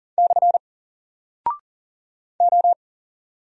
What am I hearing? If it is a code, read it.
Morse code, heard as a keyed tone, about 40 wpm, 700 Hz. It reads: / O
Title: Single Character (Letter:Number:Punctuation:Prosign) 40wpm